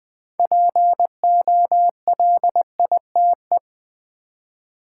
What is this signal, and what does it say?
Morse code, heard as a keyed tone, about 20 wpm, 700 Hz. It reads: POLITE
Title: Build A Word - 3 Letters At A Time - 6-Letter Words - 20wpm